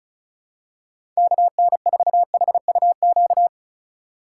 Morse code, heard as a keyed tone, about 34 wpm, 700 Hz. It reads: KN4HUQ
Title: USA Call Signs - 6 Characters - 3x Character Spacing - 35wpm